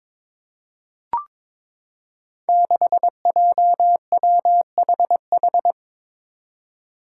Morse code, heard as a keyed tone, about 23 wpm, 700 Hz. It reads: 6JWHH
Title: Five Challenging Letter-Number Combinations 22wpm